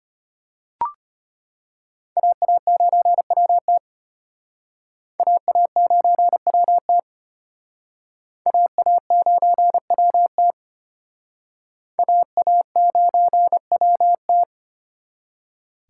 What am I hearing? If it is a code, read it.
Morse code, heard as a keyed tone, about 31 wpm, 700 Hz. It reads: AA9WT AA9WT AA9WT AA9WT
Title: Speed Racing - USA 2x2 Call Signs 25wpm